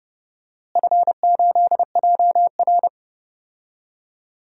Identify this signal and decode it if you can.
Morse code, heard as a keyed tone, about 30 wpm, 700 Hz. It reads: F8JR
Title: DXCC-100 Call Signs - 4 Characters - Repeated 3x - 30wpm